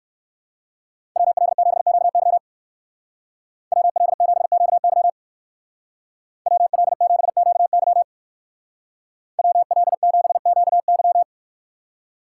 Morse code, heard as a keyed tone, about 59 wpm, 700 Hz. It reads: WL7QY WL7QY WL7QY WL7QY